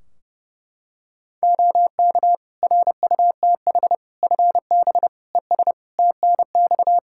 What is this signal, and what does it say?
Morse code, heard as a keyed tone, about 30 wpm, 700 Hz. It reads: OK RUTH FB ES TNX
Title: Practice QSO - Exchange 2 - Info - 30wpm